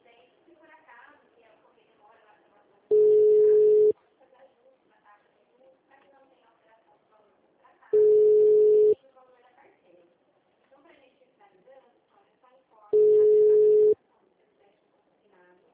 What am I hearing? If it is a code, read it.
Ringback tone (Europe, 425 Hz), with speech.
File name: out- -2006- - - .37160